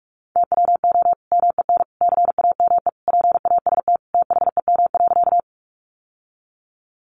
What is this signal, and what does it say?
Morse code, heard as a keyed tone, about 44 wpm, 700 Hz. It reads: TWO MEN CAME PAST THEM.
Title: Sentences from Top 600 Words 45wpm